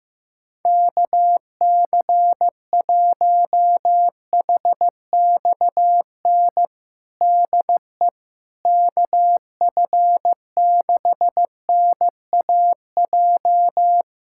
Morse code, heard as a keyed tone, about 15 wpm, 700 Hz. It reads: KC1HXN DE KF6NAJ
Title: USA Callsign DE Callsign - 6 Characters - 15wpm